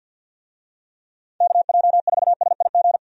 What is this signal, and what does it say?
Morse code, heard as a keyed tone, about 51 wpm, 700 Hz. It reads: KJ4SIG
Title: USA Call Signs - 6 Characters - 50wpm